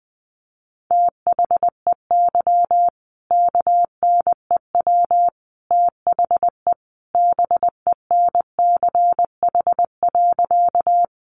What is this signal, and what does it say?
Morse code, heard as a keyed tone, about 20 wpm, 700 Hz. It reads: THEY KNEW THE BENCH.
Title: Sentences from Top 1400 Words 20wpm